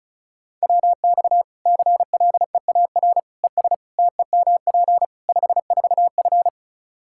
Morse code, heard as a keyed tone, about 36 wpm, 700 Hz. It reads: WX CLEAR ES TEMP 54F
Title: Practice QSO - Exchange 2 - Info - 35wpm